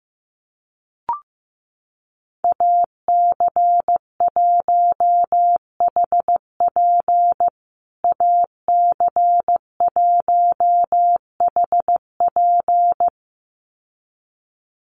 Morse code, heard as a keyed tone, about 15 wpm, 700 Hz. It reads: AC1HP AC1HP
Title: USA Call Signs Repeated 2x 15wpm